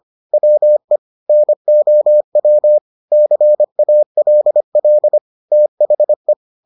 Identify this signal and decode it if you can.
Morse code, heard as a keyed tone, about 25 wpm, 600 Hz. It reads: WE NOW CALL THE